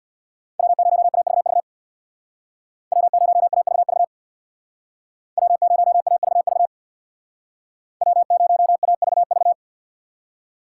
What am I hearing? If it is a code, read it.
Morse code, heard as a keyed tone, about 64 wpm, 700 Hz. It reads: W0AVV W0AVV W0AVV W0AVV